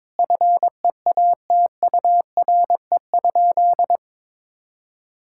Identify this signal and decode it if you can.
Morse code, heard as a keyed tone, about 22 wpm, 700 Hz. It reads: FEATURE?
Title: Sentences from Top 1000 Words - Review Part 5 - 22wpm